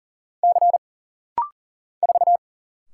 Morse code, heard as a keyed tone, about 40 wpm, 700 Hz. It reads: C 4
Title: Single Letter-Number Alt - Rapid-Fire 40wpm